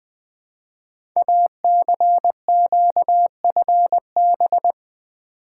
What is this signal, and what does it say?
Morse code, heard as a keyed tone, about 20 wpm, 700 Hz. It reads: ACQFB
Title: Code Groups - 5 Letters - 20wpm